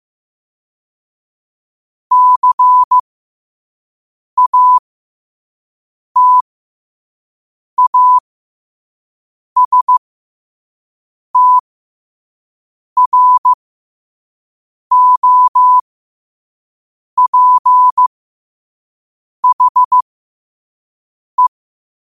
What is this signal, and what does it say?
Morse code, heard as a keyed tone, about 15 wpm, spaced as for 5 wpm, 1000 Hz. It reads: CATASTROPHE
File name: qotdfiles / 5wpm